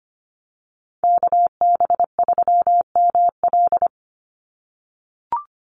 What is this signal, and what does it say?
Morse code, heard as a keyed tone, about 25 wpm, 700 Hz. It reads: KB3ML
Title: Call Signs - 25wpm